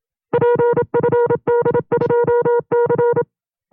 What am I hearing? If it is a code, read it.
Morse code, heard as a keyed tone, about 27 wpm, 480 Hz. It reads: PFD2C
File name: FreeFiles / CW-practice / 25WPM-side1